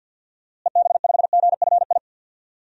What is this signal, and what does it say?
Morse code, heard as a keyed tone, about 47 wpm, 700 Hz. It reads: EB5GFI
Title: DXCC-100 Call Signs - 6 Characters - Repeated 2x - 50wpm